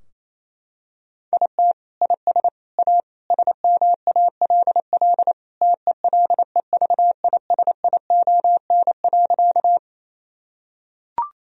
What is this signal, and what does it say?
Morse code, heard as a keyed tone, about 28 wpm, 700 Hz. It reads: IT IS A SMALL TELEVISION.
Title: Sentences from Top 1000 Words - Review Part 4 - 28wpm